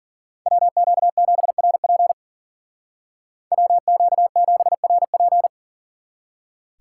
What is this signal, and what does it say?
Morse code, heard as a keyed tone, about 50 wpm, 700 Hz. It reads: WQ7RP WQ7RP
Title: Speed Racing - USA 2x2 Call Signs 40wpm